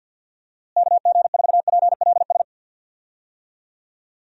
Morse code, heard as a keyed tone, about 47 wpm, 700 Hz. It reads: KG4PLS
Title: USA Call Signs - 6 Characters - Repeated 3x - 50wpm